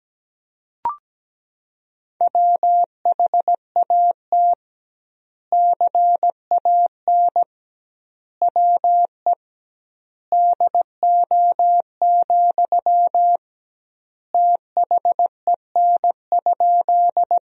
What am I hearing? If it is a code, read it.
Morse code, heard as a keyed tone, about 17 wpm, 700 Hz. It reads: WHAT CAN WE DO, THEN?
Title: 2x Word-Spacing - Sentences from Top 100 Words - 17wpm